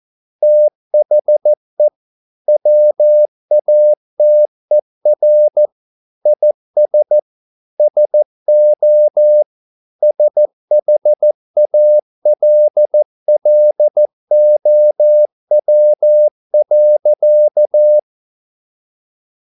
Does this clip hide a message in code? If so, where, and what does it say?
Yes: Morse code from 0.4 to 18.0 s, heard as a keyed tone, about 14 wpm, 600 Hz. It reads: THE WATER IS SO SHALLOW.